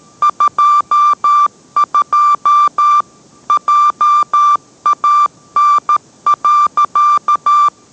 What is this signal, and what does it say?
Morse code, heard as a keyed tone, about 13 wpm, 1180 Hz. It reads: 22 JAN.